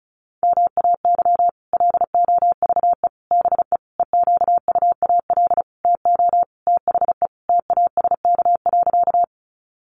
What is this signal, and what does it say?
Morse code, heard as a keyed tone, about 34 wpm, 700 Hz. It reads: MAY LOVE BE EQUAL TO THE TASK.
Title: Sentences from Top 1400 Words 35wpm